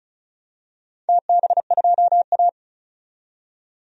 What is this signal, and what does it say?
Morse code, heard as a keyed tone, about 36 wpm, 700 Hz. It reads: TB2A
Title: DXCC-100 Call Signs - 4 Characters - 35wpm